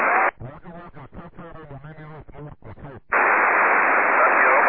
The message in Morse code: none